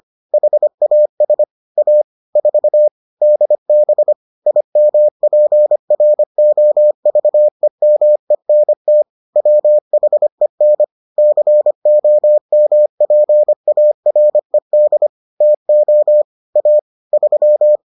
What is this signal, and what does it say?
Morse code, heard as a keyed tone, about 25 wpm, 600 Hz. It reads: HAS A 4 DB IMPROVEMENT WHEN COMPARED TO A 3